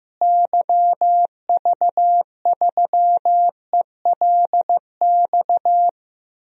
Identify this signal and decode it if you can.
Morse code, heard as a keyed tone, about 15 wpm, 700 Hz. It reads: YV3ELX
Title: DXCC-100 Call Signs - 6 Characters - Repeated 2x - 15wpm